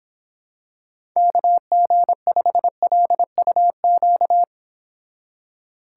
Morse code, heard as a keyed tone, about 26 wpm, 700 Hz. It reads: KG5LUQ